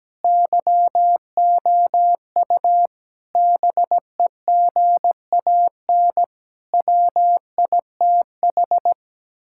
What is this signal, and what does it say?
Morse code, heard as a keyed tone, about 17 wpm, 700 Hz. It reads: YOU BEGAN WITH
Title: Sentences from Top 1600 Words 17wpm